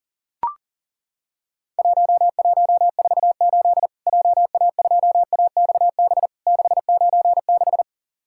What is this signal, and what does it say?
Morse code, heard as a keyed tone, about 40 wpm, 700 Hz. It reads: 1148 JA2AXB 696
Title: Worked All Europe DX Contest - QTC Exchange - 40wpm